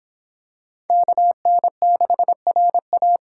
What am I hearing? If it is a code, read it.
Morse code, heard as a keyed tone, about 26 wpm, 700 Hz. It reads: KN6RA